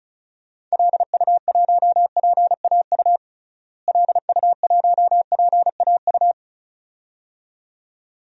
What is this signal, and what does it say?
Morse code, heard as a keyed tone, about 36 wpm, 700 Hz. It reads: LU1PAU LU1PAU
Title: DXCC-100 Call Signs - 6 Characters - Repeated 2x - 35wpm